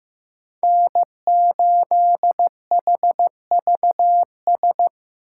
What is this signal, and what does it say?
Morse code, heard as a keyed tone, about 15 wpm, 700 Hz. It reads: N8HVS
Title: USA Call Signs - 5 Characters - 8x Character Spacing - 15wpm